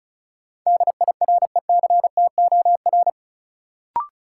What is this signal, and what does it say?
Morse code, heard as a keyed tone, about 36 wpm, 700 Hz. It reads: DIRECTOR
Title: Sets of 3 Words - New Word Drill - 35wpm